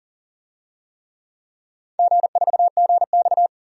Morse code, heard as a keyed tone, about 40 wpm, 700 Hz. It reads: G4GX
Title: DXCC-100 Call Signs - 4 Characters - 40wpm